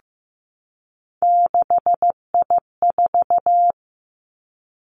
Morse code, heard as a keyed tone, about 15 wpm, 700 Hz. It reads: 6I4
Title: Three Challenging Letter-Number Combinations 15wpm